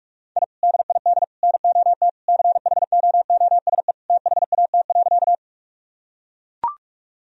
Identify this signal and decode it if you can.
Morse code, heard as a keyed tone, about 42 wpm, 700 Hz. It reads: I DID NOT CHOOSE THAT.
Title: Sentences from Top 1000 Words - Review Part 4 - 45wpm